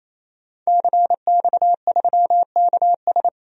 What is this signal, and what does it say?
Morse code, heard as a keyed tone, about 27 wpm, 700 Hz. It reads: CX3KS
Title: DXCC-100 Call Signs - 5 Characters - 28wpm